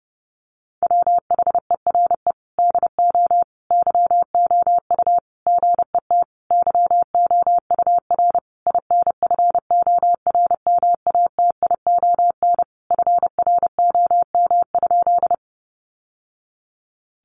Morse code, heard as a keyed tone, about 30 wpm, 700 Hz. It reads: WHERE DO YOU GET YOUR INFORMATION FROM?
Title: Sentences from Top 600 Words 30wpm